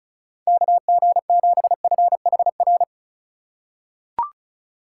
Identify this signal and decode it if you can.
Morse code, heard as a keyed tone, about 36 wpm, 690 Hz. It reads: KG7FHR
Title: USA Call Signs - 6 Characters - 6x Character Spacing - 35wpm